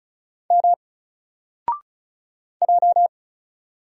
Morse code, heard as a keyed tone, about 35 wpm, 700 Hz. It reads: M J
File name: Single-Letter-Number-35wpm